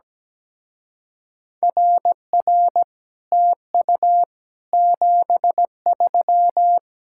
Morse code, heard as a keyed tone, about 17 wpm, 700 Hz. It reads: RR TU 73
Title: Speed-Racing Common QSO Phrases - 17wpm